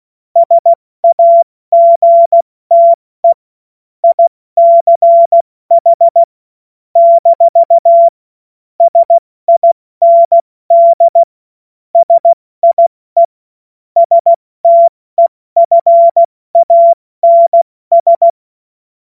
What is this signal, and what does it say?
Morse code, heard as a keyed tone, about 16 wpm, 680 Hz. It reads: SAGTE ICH - SIND SIE STEFANS